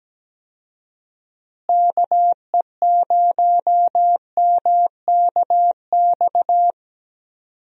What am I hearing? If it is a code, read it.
Morse code, heard as a keyed tone, about 17 wpm, 700 Hz. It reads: KE0MKX
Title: USA Call Signs - 6 Characters - Repeated 3x - 17wpm